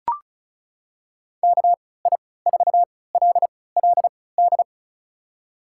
Morse code, heard as a keyed tone, about 34 wpm, spaced as for 20 wpm, 700 Hz. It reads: KI4LLD
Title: USA Call Signs - 3x Character Spacing - 35wpm